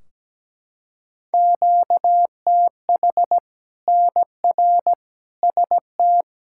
Morse code, heard as a keyed tone, about 17 wpm, 700 Hz. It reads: QTH NR ST
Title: K1USN SST QSO Practice 17wpm